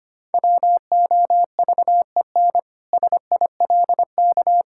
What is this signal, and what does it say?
Morse code, heard as a keyed tone, about 25 wpm, 700 Hz. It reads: WOVEN SILK
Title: Pangram Challenge v2 - 25wpm